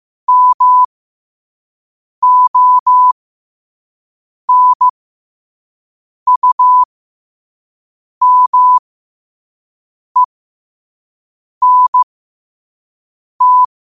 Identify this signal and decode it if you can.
Morse code, heard as a keyed tone, about 15 wpm, spaced as for 5 wpm, 1000 Hz. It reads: MONUMENT
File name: qotdfiles / 5wpm